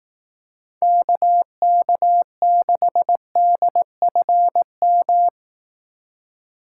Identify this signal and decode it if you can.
Morse code, heard as a keyed tone, about 18 wpm, 700 Hz. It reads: KK6DFM
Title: Speed Racing - USA 2x3 Call Signs 15wpm